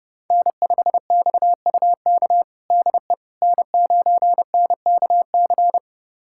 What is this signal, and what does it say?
Morse code, heard as a keyed tone, about 30 wpm, 700 Hz. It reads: N5XUK DE N9NKC
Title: USA Callsign DE Callsign - 5 Characters - 30wpm